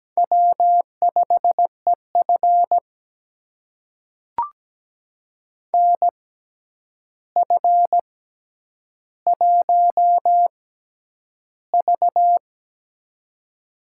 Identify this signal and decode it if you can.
Morse code, heard as a keyed tone, about 17 wpm, spaced as for 9 wpm, 700 Hz. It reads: W5EF NF1V
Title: USA Call Signs - 4 Characters - 6x Character Spacing - 17wpm